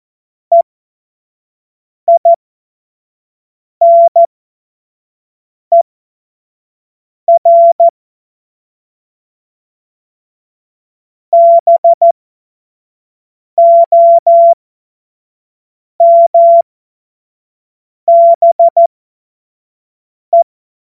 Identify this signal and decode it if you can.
Morse code, heard as a keyed tone, about 14 wpm, spaced as for 5 wpm, 680 Hz. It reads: EINER BOMBE
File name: Morsekurs / Witze / Witz 08. November 2022 25 F70